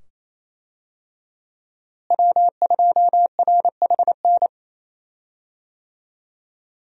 Morse code, heard as a keyed tone, about 27 wpm, 700 Hz. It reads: W2RHN
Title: USA Call Signs - 5 Characters - Repeated 2x - 28wpm